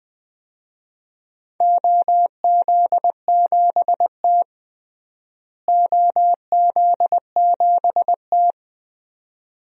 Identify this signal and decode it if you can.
Morse code, heard as a keyed tone, about 20 wpm, 700 Hz. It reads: OZ7T OZ7T
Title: DXCC-100 Call Signs - 4 Characters - Repeated 2x - 20wpm